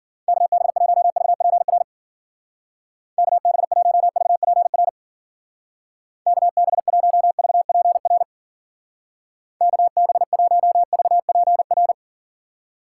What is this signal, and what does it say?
Morse code, heard as a keyed tone, about 51 wpm, 700 Hz. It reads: KB1VPR KB1VPR KB1VPR KB1VPR